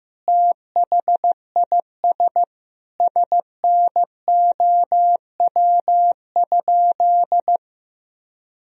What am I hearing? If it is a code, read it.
Morse code, heard as a keyed tone, about 15 wpm, 700 Hz. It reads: THIS SNOW?
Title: Sentences from Top 1000 Words - Review Part 3 - 15wpm